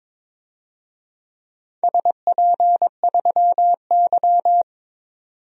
Morse code, heard as a keyed tone, about 22 wpm, 700 Hz. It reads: SP3Y
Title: DXCC-100 Call Signs - 4 Characters - 22wpm